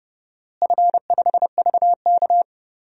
Morse code, heard as a keyed tone, about 30 wpm, 700 Hz. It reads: F5VK
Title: DXCC-100 Call Signs - 4 Characters - Repeated 2x - 30wpm